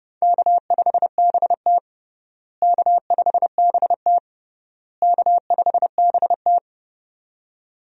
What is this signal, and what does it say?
Morse code, heard as a keyed tone, about 30 wpm, 700 Hz. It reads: K5BT K5BT K5BT
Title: USA Call Signs - 4 Characters - Repeated 3x - 30wpm